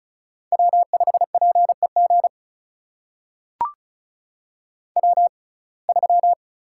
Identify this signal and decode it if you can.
Morse code, heard as a keyed tone, about 34 wpm, spaced as for 25 wpm, 700 Hz. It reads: W5PEG W3
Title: USA Call Signs - 6x Character Spacing - 35wpm